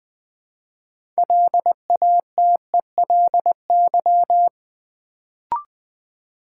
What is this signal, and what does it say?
Morse code, heard as a keyed tone, about 20 wpm, 700 Hz. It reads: LATELY
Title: Build A Word - 3 Letters At A Time - 6-Letter Words - 20wpm